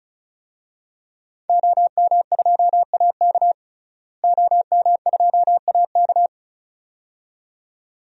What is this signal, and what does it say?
Morse code, heard as a keyed tone, about 36 wpm, 700 Hz. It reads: OM2AK OM2AK
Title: DXCC-100 Call Signs - 5 Characters - Repeated 2x - 35wpm